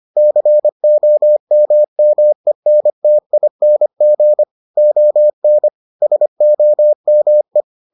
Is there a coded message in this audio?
Yes: Morse code from 0.2 to 7.6 s, heard as a keyed tone, about 25 wpm, 600 Hz. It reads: COMMENTING ON SOME